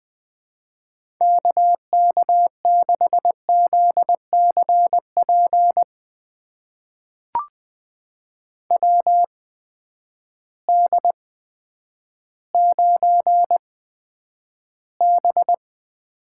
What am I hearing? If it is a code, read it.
Morse code, heard as a keyed tone, about 20 wpm, spaced as for 10 wpm, 700 Hz. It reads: KK6ZCP WD9B
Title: USA Call Signs - 8x Character Spacing - 20wpm